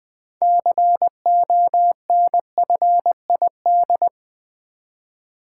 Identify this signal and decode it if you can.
Morse code, heard as a keyed tone, about 20 wpm, 700 Hz. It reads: CONFID
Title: Build A Word - 3 Letters At A Time - 9-Letter Words - 20wpm